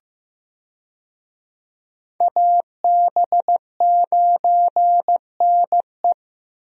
Morse code, heard as a keyed tone, about 16 wpm, 700 Hz. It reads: AB9NE
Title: USA Call Signs - 5 Characters - Repeated 3x - 15wpm